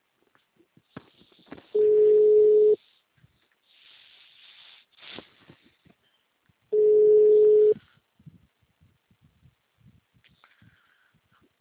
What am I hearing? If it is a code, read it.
Ringback tone (Europe, 425 Hz).
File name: out- -2007- - - .251